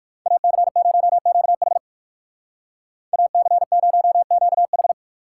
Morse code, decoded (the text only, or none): AC0QH AC0QH